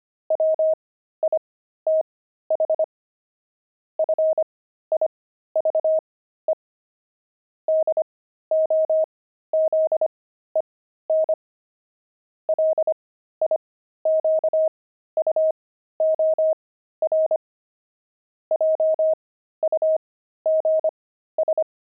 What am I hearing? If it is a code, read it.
Morse code, heard as a keyed tone, about 25 wpm, spaced as for 13 wpm, 630 Hz. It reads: WITH FIVE DOZEN LIQUOR JUGS
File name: pangram-fs-13-cs-25-jugs